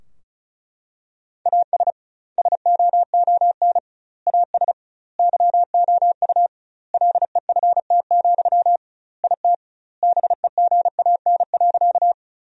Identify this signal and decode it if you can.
Morse code, heard as a keyed tone, about 36 wpm, 700 Hz. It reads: AS SOON AS YOU LEFT, IT BEGAN.
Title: 2x Word-Spacing - Sentences from Top 400 Words - 35wpm